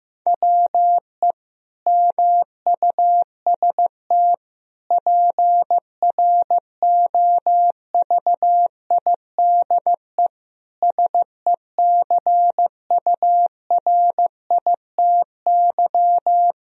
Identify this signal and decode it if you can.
Morse code, heard as a keyed tone, about 16 wpm, 700 Hz. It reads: WE MUST PROVIDE SECURITY
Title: Sentences from Top 1000 Words - Review Part 4 - 15wpm